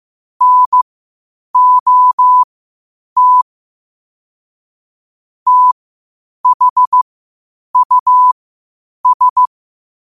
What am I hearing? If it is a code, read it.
Morse code, heard as a keyed tone, about 15 wpm, spaced as for 8 wpm, 1000 Hz. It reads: NOT THUS
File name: qotdfiles / 8wpm